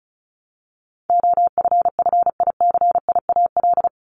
Morse code, heard as a keyed tone, about 34 wpm, 700 Hz. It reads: OFFICIAL
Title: Top 751-1000 Words 35wpm